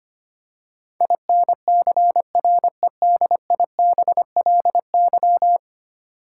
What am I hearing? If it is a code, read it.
Morse code, heard as a keyed tone, about 25 wpm, 700 Hz. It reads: INCREDIBLY
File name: 10 Letter Words - First 3 Letters Given - 25wpm